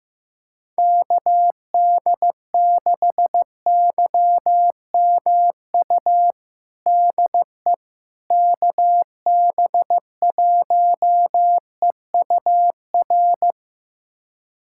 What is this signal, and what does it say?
Morse code, heard as a keyed tone, about 15 wpm, 700 Hz. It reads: KD6YMU DE KB1EUR
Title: USA Callsign DE Callsign - 6 Characters - 15wpm